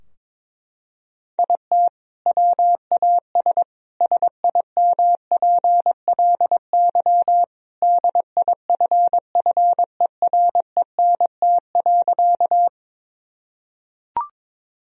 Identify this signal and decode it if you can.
Morse code, heard as a keyed tone, about 22 wpm, 700 Hz. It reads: IT WAS SIMPLY DIFFERENT.
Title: Sentences from Top 1000 Words - Review Part 4 - 22wpm